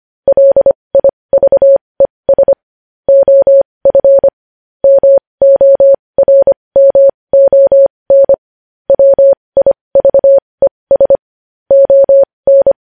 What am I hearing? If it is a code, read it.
Morse code, heard as a keyed tone, about 25 wpm, spaced as for 20 wpm, 550 Hz. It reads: LIVES OF MORMON WIVES ON
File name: podcast / News headlines-1x-25x20